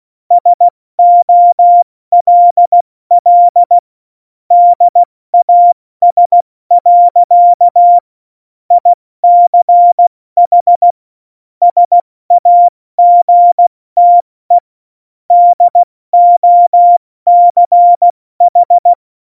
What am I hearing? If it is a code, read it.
Morse code, heard as a keyed tone, about 16 wpm, 700 Hz. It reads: SOLL DAS. ICH SAGTE DOCH
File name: Morsekurs / Witze / Witz 16. Januar 2018, 70 F80